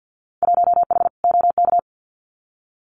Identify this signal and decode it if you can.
Morse code, heard as a keyed tone, about 50 wpm, 700 Hz. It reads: 1H OK
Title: Winter Field Day Exchanges 50wpm